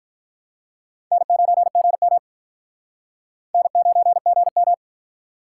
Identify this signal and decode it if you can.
Morse code, heard as a keyed tone, about 48 wpm, 700 Hz. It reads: N9GM N9GM